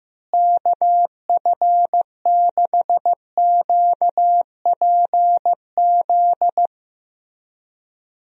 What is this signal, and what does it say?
Morse code, heard as a keyed tone, about 15 wpm, 700 Hz. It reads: KF6QPZ
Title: USA Call Signs - 6 Characters - Repeated 3x - 15wpm